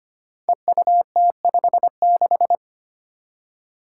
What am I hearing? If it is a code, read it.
Morse code, heard as a keyed tone, about 25 wpm, 700 Hz. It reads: EUT56
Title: Code Groups - 5 Letter:Numbers - 25wpm